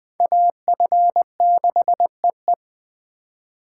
Morse code, heard as a keyed tone, about 21 wpm, 700 Hz. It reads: AF6EE
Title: Speed Racing - USA 2x2 Call Signs 17wpm